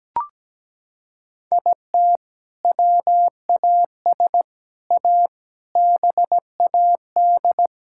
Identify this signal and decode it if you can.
Morse code, heard as a keyed tone, about 17 wpm, 700 Hz. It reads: IT WAS A BAD
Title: Sentences from Top 1000 Words - Review Part 5 - 17wpm